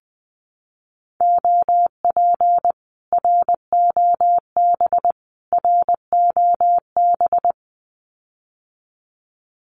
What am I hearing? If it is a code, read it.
Morse code, heard as a keyed tone, about 20 wpm, 700 Hz. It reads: OP ROB ROB
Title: Common QSO Phrases Encore 20wpm